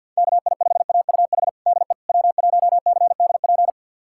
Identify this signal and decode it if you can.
Morse code, heard as a keyed tone, about 52 wpm, 700 Hz. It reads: KI5AUH DE W1CDP
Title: USA Callsign DE Callsign 50wpm